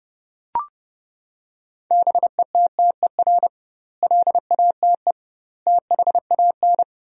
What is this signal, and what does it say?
Morse code, heard as a keyed tone, about 30 wpm, 700 Hz. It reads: BETTER LATE THAN
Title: 2x Word-Spacing - Sentences from Top 400 Words - 30wpm